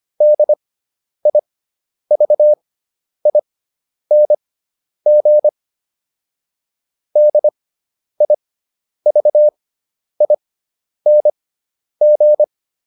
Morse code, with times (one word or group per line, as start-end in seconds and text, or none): 0.2-5.5 DIVING
7.2-12.4 DIVING